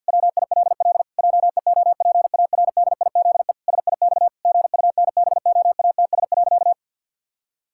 Morse code, heard as a keyed tone, about 53 wpm, 700 Hz. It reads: WILL JEOPARDIZE SIX GUNBOATS.
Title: Pangram Challenge v2 - 50wpm